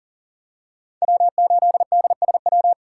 Morse code, heard as a keyed tone, about 40 wpm, 700 Hz. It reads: W8DSW
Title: USA Call Signs - 5 Characters - 2x Character Spacing - 40wpm